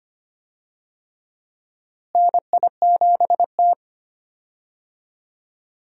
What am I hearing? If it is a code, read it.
Morse code, heard as a keyed tone, about 25 wpm, 700 Hz. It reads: NI7T
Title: USA Call Signs - 4 Characters - Repeated 3x - 25wpm